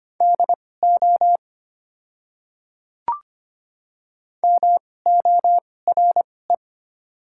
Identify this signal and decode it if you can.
Morse code, heard as a keyed tone, about 25 wpm, spaced as for 18 wpm, 700 Hz. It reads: DO MORE
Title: 2x Character Spacing - Top 100 Words - 25wpm